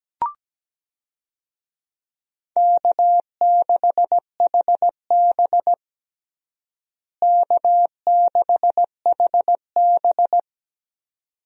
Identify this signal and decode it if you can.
Morse code, heard as a keyed tone, about 17 wpm, 700 Hz. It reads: K6HB K6HB
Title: USA Call Signs - 4 Characters - Repeated 2x - 17wpm